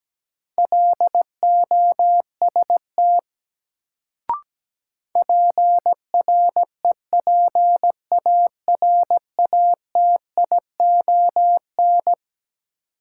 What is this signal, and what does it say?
Morse code, heard as a keyed tone, about 17 wpm, 700 Hz. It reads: LOST PREPARATION
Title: Sentences from Top 1400 Words - New Word Drill - 17wpm